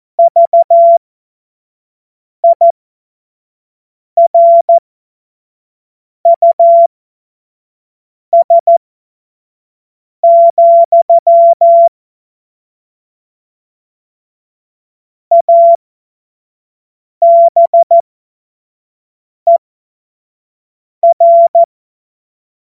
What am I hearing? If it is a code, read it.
Morse code, heard as a keyed tone, about 14 wpm, spaced as for 5 wpm, 680 Hz. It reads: VIRUS, ABER
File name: Morsekurs / Witze / Witz 03. August 2021, 25 F70